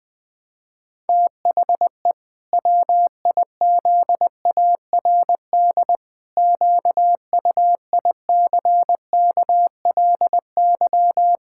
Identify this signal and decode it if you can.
Morse code, heard as a keyed tone, about 20 wpm, 700 Hz. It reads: THE WIZARD QUICKLY
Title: Pangram Challenge v2 - 20wpm